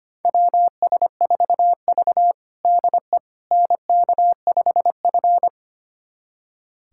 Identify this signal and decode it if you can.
Morse code, heard as a keyed tone, about 25 wpm, 700 Hz. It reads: WS4V DE NK5F
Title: USA Callsign DE Callsign - 4 Characters - 25wpm